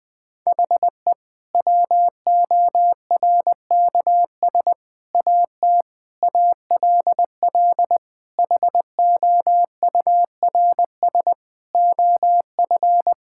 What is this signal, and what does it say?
Morse code, heard as a keyed tone, about 19 wpm, 700 Hz. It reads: HE WORKS AT ALL HOURS OF
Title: Sentences with 8-Word Phrases - 20wpm